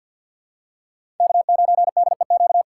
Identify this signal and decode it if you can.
Morse code, heard as a keyed tone, about 50 wpm, 700 Hz. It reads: K9DEQ
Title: USA Call Signs - 5 Characters - 6x Character Spacing - 50wpm